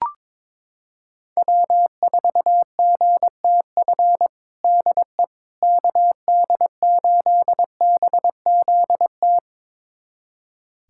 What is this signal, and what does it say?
Morse code, heard as a keyed tone, about 21 wpm, 700 Hz. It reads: W4GTF DE KD8BZT
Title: USA Callsign DE Callsign 22wpm